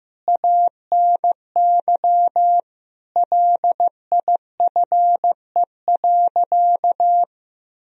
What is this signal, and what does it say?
Morse code, heard as a keyed tone, about 15 wpm, 700 Hz. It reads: ANY LIFE.
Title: Sentences from Top 300 Words and Abbreviations 15wpm